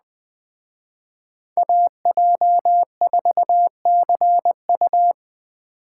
Morse code, heard as a keyed tone, about 20 wpm, 700 Hz. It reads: AJ4CU
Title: Speed Racing - USA Call Signs 17wpm